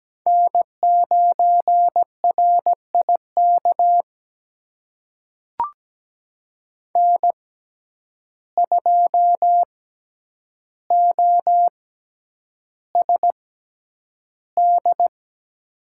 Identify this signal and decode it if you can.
Morse code, heard as a keyed tone, about 17 wpm, spaced as for 9 wpm, 700 Hz. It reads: N9RIK N2OSD
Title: USA Call Signs - 5 Characters - 6x Character Spacing - 17wpm